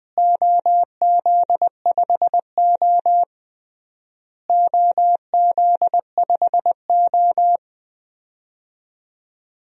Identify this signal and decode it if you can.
Morse code, heard as a keyed tone, about 20 wpm, 700 Hz. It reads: OZ5O OZ5O
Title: DXCC-100 Call Signs - 4 Characters - Repeated 3x - 20wpm